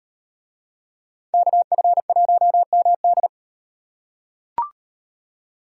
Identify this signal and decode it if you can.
Morse code, heard as a keyed tone, about 36 wpm, 700 Hz. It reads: KF1MD